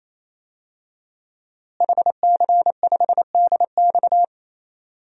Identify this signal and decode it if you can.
Morse code, heard as a keyed tone, about 28 wpm, 700 Hz. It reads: HC5DX
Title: DXCC-100 Call Signs - 5 Characters - Repeated 3x - 28wpm